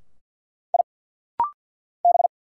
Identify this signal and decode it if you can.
Morse code, heard as a keyed tone, about 50 wpm, 700 Hz. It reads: I B
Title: Single Letters - Rapid-Fire 50wpm